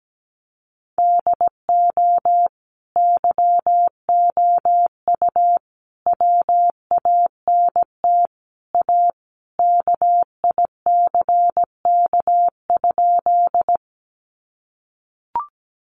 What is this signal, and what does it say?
Morse code, heard as a keyed tone, about 17 wpm, 700 Hz. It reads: DO YOU WANT A KICK?
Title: Sentences from Top 1300 Words 17wpm